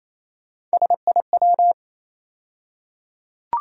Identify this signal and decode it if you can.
Morse code, heard as a keyed tone, about 28 wpm, 700 Hz. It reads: SIW
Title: USA Call Sign Postfixes - 3 Letters 28wpm